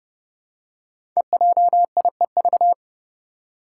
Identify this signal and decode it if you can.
Morse code, heard as a keyed tone, about 32 wpm, 700 Hz. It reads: EJIEV
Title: Five Challenging Letter-Number Combinations 30wpm